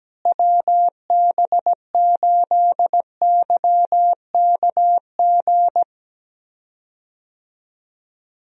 Morse code, heard as a keyed tone, about 17 wpm, 700 Hz. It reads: WB8YKG
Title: USA Call Signs - 6 Characters - Repeated 2x - 17wpm